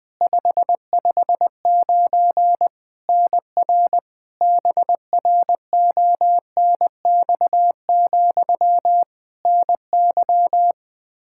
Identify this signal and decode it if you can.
Morse code, heard as a keyed tone, about 20 wpm, 700 Hz. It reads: 559 NR BRONX, NY
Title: K1USN SST QSO Practice 20wpm